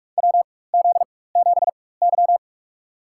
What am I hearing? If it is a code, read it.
Morse code, heard as a keyed tone, about 46 wpm, spaced as for 21 wpm, 700 Hz. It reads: WZ7Y
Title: USA Call Signs - 4 Characters - 4x Character Spacing - 45wpm